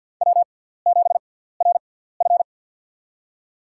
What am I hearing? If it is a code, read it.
Morse code, heard as a keyed tone, about 49 wpm, spaced as for 17 wpm, 700 Hz. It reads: W7RF
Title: USA Call Signs - 4 Characters - 6x Character Spacing - 50wpm